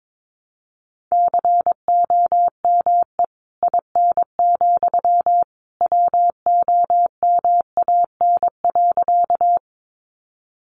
Morse code, heard as a keyed tone, about 22 wpm, 700 Hz. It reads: COME IN, WOMAN.
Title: Sentences from Top 1100 Words 22wpm